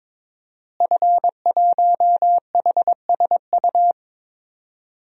Morse code, heard as a keyed tone, about 22 wpm, 700 Hz. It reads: F1HSU
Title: DXCC-100 Call Signs - 5 Characters - Repeated 2x - 22wpm